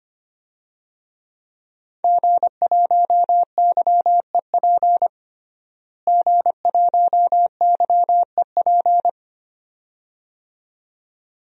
Morse code, heard as a keyed tone, about 25 wpm, 700 Hz. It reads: G1YEP G1YEP
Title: DXCC-100 Call Signs - 5 Characters - Repeated 2x - 25wpm